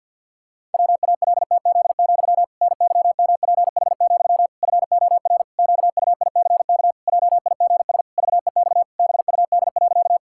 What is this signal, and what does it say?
Morse code, heard as a keyed tone, about 50 wpm, 700 Hz. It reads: WALTZ, NYMPH, FOR QUICK JIGS VEX BUD.
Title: Pangram Challenge v2 - 50wpm